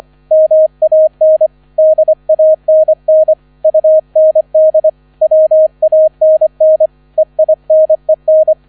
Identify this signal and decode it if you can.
Morse code, heard as a keyed tone, about 25 wpm, 630 Hz. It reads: MAN DANN UND WANN EINEN